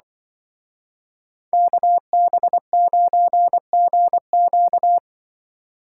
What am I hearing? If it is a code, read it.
Morse code, heard as a keyed tone, about 24 wpm, 700 Hz. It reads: KB9GQ
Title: Speed Racing - USA 2x2 Call Signs 20wpm